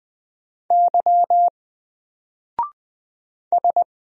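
Morse code, heard as a keyed tone, about 20 wpm, 700 Hz. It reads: Y S
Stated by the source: Single letter and number Morse drill at 20 wpm, from N4LCD.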